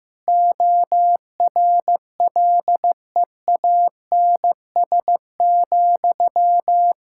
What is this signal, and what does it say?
Morse code, heard as a keyed tone, about 16 wpm, 700 Hz. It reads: ORLEANS,
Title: Practice QSO - Exchange 1 - Beginning - 15wpm